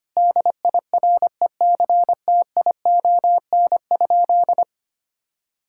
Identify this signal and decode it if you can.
Morse code, heard as a keyed tone, about 25 wpm, 700 Hz. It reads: DIRECTION?
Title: Sentences from Top 1000 Words - Review Part 5 - 25wpm